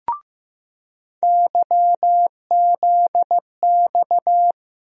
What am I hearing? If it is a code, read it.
Morse code, heard as a keyed tone, about 15 wpm, 700 Hz. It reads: YZX
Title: USA Call Sign Postfixes - 3 Letters 15wpm